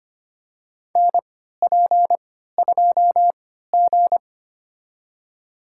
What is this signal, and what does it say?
Morse code, heard as a keyed tone, about 25 wpm, spaced as for 14 wpm, 700 Hz. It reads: NP2G
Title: USA Call Signs - 4 Characters - 3x Character Spacing - 25wpm